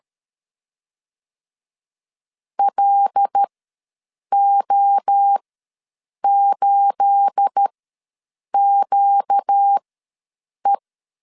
Morse code, heard as a keyed tone, about 13 wpm, spaced as for 7 wpm, 800 Hz. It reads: LO8QE
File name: LEZIONI / 9 28 ws 12wpm 3354